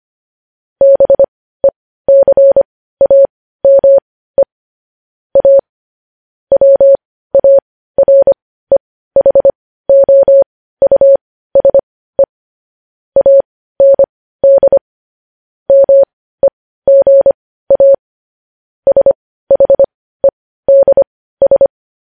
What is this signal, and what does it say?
Morse code, heard as a keyed tone, about 25 wpm, spaced as for 15 wpm, 550 Hz. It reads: BECAME A WAREHOUSE AND MEGA SHEDS
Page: News Headlines in Morse Code at 15 WPM